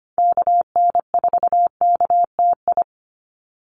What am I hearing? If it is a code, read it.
Morse code, heard as a keyed tone, about 25 wpm, 700 Hz. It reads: KN4KTI
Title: USA Call Signs 25wpm